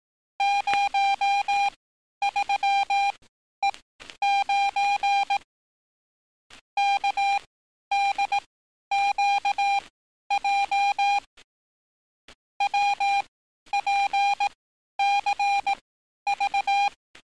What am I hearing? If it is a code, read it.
Morse code, heard as a keyed tone, about 18 wpm, spaced as for 11 wpm, 800 Hz. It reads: 03E9 KDQJ WPCV
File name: training / NAVPERS 11049to11060 / NAVPERS 11051 RA RB